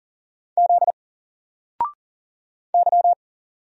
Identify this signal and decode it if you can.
Morse code, heard as a keyed tone, about 40 wpm, 700 Hz. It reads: Z Y
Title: Single Character (Letter:Number:Punctuation:Prosign) Alt 40wpm